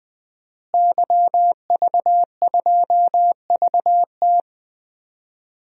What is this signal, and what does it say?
Morse code, heard as a keyed tone, about 20 wpm, 700 Hz. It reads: YV2VT
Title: DXCC-100 Call Signs - 5 Characters - Repeated 3x - 20wpm